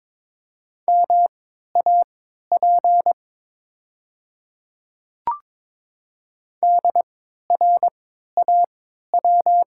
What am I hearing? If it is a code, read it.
Morse code, heard as a keyed tone, about 22 wpm, spaced as for 12 wpm, 700 Hz. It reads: MAP DRAW
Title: 3x Character Spacing - Top 500 Words - 22wpm